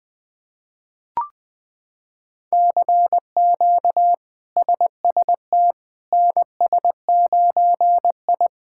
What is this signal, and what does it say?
Morse code, heard as a keyed tone, about 20 wpm, 700 Hz. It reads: CQ SST NS9I
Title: K1USN SST QSO Practice - 20wpm